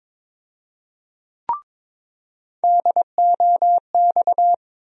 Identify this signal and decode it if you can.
Morse code, heard as a keyed tone, about 21 wpm, 700 Hz. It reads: DOX
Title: USA Call Sign Postfixes - 3 Letters 22wpm